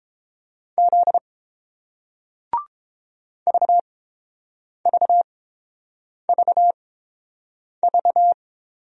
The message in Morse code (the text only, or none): Z VVVV